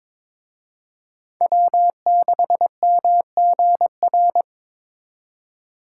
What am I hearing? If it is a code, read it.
Morse code, heard as a keyed tone, about 26 wpm, 700 Hz. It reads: W6MGR
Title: Speed Racing - USA 1x3 Call Signs 22wpm